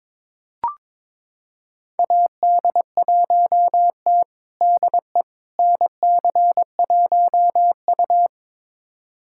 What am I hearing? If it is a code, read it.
Morse code, heard as a keyed tone, about 21 wpm, 700 Hz. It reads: AD1T DE NC1U
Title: USA Callsign DE Callsign - 4 Characters - 22wpm